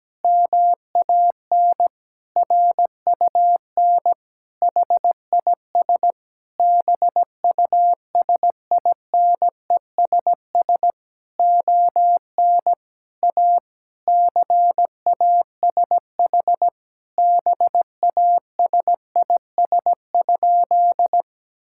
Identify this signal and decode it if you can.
Morse code, heard as a keyed tone, about 17 wpm, 700 Hz. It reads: MAN RUN HIS BUSINESS ON A CASH BASIS?
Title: Sentences from Top 1000 Words - Review Part 5 - 17wpm